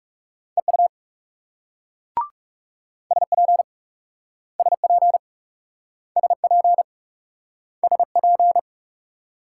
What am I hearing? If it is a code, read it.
Morse code, heard as a keyed tone, about 40 wpm, 700 Hz. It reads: EU SP SP SP SP